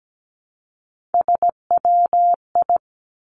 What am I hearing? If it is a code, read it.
Morse code, heard as a keyed tone, about 17 wpm, 700 Hz. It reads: SWI
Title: Three Challenging Letter-Number Combinations 17wpm